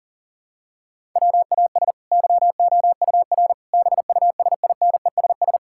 Morse code, heard as a keyed tone, about 40 wpm, 700 Hz. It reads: WAS YOUR BUSINESS
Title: Sentences from Top 1000 Words - Review Part 5 - 40wpm